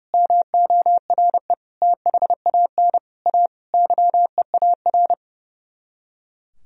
Morse code, heard as a keyed tone, about 37 wpm, 700 Hz. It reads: MORE THAN A YEAR
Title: Speed-Racing - Sets of 5 Words 30wpm